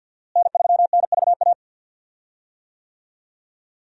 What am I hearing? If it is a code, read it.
Morse code, heard as a keyed tone, about 51 wpm, 700 Hz. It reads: N3NVA
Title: USA Call Signs - 5 Characters - Repeated 3x - 50wpm